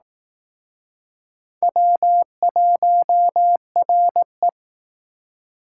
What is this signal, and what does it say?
Morse code, heard as a keyed tone, about 19 wpm, 700 Hz. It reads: W1RE
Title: Speed Racing - USA Call Signs 15wpm